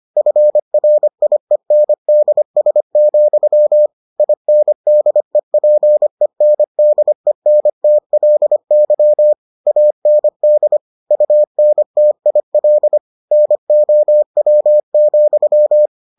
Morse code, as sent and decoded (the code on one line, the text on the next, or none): ..-. .-. .. -. -.. ... --..-- / .. -. -.. .--. -. -.. -. - .-.. -.-- / .- -. -.. / ..- -. - .. .-.. / -. --- .-- --..--
FRIENDS, INDEPENDENTLY AND UNTIL NOW,